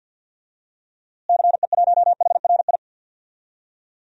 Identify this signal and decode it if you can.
Morse code, heard as a keyed tone, about 51 wpm, 700 Hz. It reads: CE1HRI
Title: DXCC-100 Call Signs - 6 Characters - Repeated 2x - 50wpm